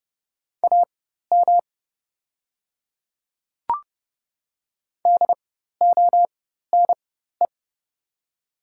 Morse code, heard as a keyed tone, about 30 wpm, spaced as for 14 wpm, 690 Hz. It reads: AM DONE
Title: 4x Character Spacing - Top 500 Words - 30wpm